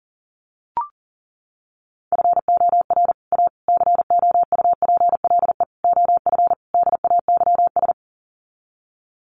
Morse code, heard as a keyed tone, about 40 wpm, 700 Hz. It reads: FOR A COUPLE OF DAYS
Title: Sets of 5 Words Encore2 40wpm